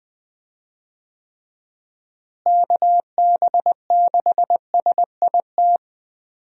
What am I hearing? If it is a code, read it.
Morse code, heard as a keyed tone, about 21 wpm, 700 Hz. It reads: KB6SIT
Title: USA Call Signs - 6 Characters - Repeated 3x - 20wpm